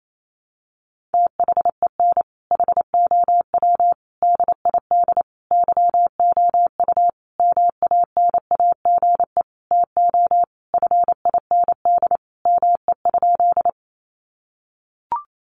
Morse code, heard as a keyed tone, about 28 wpm, 700 Hz. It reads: THEN HOW DID YOU MANAGE TO FIND ME?
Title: Sentences from Top 1400 Words 28wpm